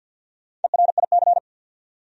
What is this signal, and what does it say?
Morse code, heard as a keyed tone, about 52 wpm, 700 Hz. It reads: ERIC
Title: USA Common Names - Max 4 Letters - 50wpm